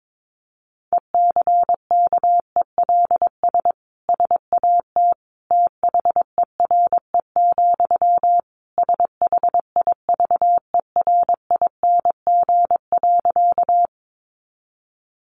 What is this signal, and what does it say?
Morse code, heard as a keyed tone, about 22 wpm, 700 Hz. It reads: ECKELS SAT THERE, SHIVERING.
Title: A Sound Of Thunder 22wpm